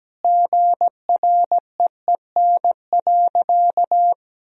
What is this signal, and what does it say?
Morse code, heard as a keyed tone, about 17 wpm, 700 Hz. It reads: GREEN.
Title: Sentences from Top 1000 Words - Review Part 2 - 17wpm